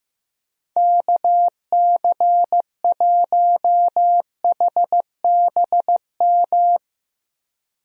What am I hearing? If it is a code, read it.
Morse code, heard as a keyed tone, about 15 wpm, 700 Hz. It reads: KC1HBM
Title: USA Call Signs - 6 Characters - Repeated 3x - 15wpm